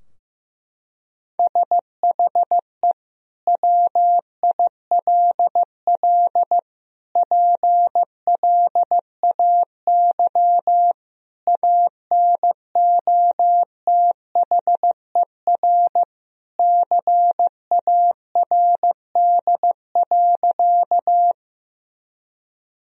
Morse code, heard as a keyed tone, about 15 wpm, 700 Hz. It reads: SHE WILL PLAY ANOTHER CARD.
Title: Sentences from Top 1000 Words - Review Part 4 - 15wpm